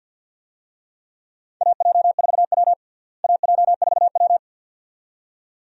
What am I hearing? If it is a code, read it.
Morse code, heard as a keyed tone, about 46 wpm, 700 Hz. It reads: AJ4W AJ4W
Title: USA Call Signs - 4 Characters - Repeated 2x - 50wpm